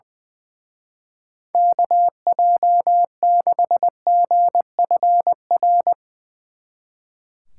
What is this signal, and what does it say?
Morse code, heard as a keyed tone, about 22 wpm, 700 Hz. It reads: KJ6GFR
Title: Speed Racing - USA 2x3 Call Signs 20wpm